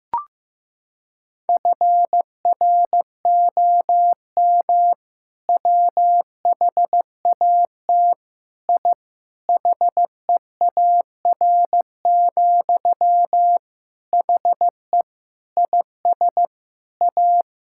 Morse code, heard as a keyed tone, about 15 wpm, 700 Hz. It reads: FROM WHAT I HEAR, HE IS A
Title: Sentences from Top 1000 Words - Review Part 6 - 15wpm